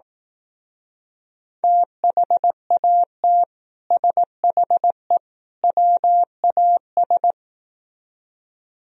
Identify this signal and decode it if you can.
Morse code, heard as a keyed tone, about 18 wpm, 700 Hz. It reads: THAT SHE WAS
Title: Speed-Racing - Sets of 3 Words 15wpm